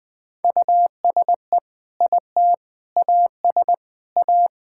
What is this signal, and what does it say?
Morse code, heard as a keyed tone, about 21 wpm, 700 Hz. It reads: USE IT AS A
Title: Sentences with 4-Word Phrases - 20wpm